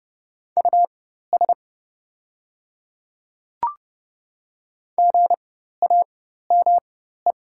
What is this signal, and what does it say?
Morse code, heard as a keyed tone, about 29 wpm, spaced as for 14 wpm, 700 Hz. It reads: US GAME
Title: 4x Character Spacing - Top 500 Words - 30wpm